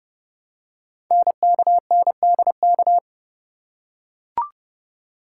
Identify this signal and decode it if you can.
Morse code, heard as a keyed tone, about 30 wpm, 700 Hz. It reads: NKNDK
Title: Code Groups - 5 Letters - 30wpm